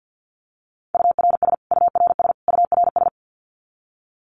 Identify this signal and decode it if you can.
Morse code, heard as a keyed tone, about 50 wpm, 700 Hz. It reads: URS URS URS
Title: Abbreviations Repeated 3x - 50wpm